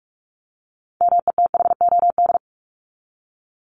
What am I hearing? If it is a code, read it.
Morse code, heard as a keyed tone, about 44 wpm, 700 Hz. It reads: METHOD
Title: Top 501-750 Words 45wpm